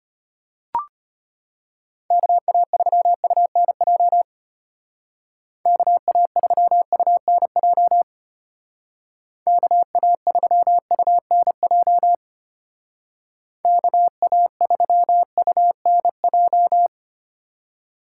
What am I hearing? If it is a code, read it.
Morse code, heard as a keyed tone, about 33 wpm, 700 Hz. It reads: KA3UNJ KA3UNJ KA3UNJ KA3UNJ